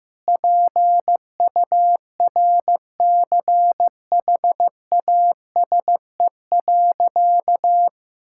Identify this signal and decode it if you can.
Morse code, heard as a keyed tone, about 15 wpm, 700 Hz. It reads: PURCHASE.
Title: Sentences from Top 1000 Words - Review Part 5 - 15wpm